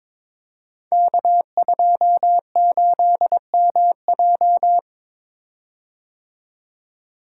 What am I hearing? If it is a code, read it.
Morse code, heard as a keyed tone, about 22 wpm, 700 Hz. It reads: K28MJ
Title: Code Groups - 5 Letter:Numbers - 22wpm